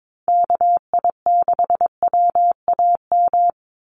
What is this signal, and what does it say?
Morse code, heard as a keyed tone, about 22 wpm, 700 Hz. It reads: KI6WAM
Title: USA Call Signs 22wpm